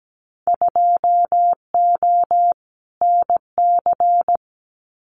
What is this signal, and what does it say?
Morse code, heard as a keyed tone, about 17 wpm, 700 Hz. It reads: 2O NC
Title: Winter Field Day Exchanges 17wpm